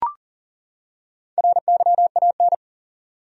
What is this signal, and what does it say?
Morse code, heard as a keyed tone, about 40 wpm, 700 Hz. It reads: RYAN
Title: USA Common Names - Max 4 Letters - 40wpm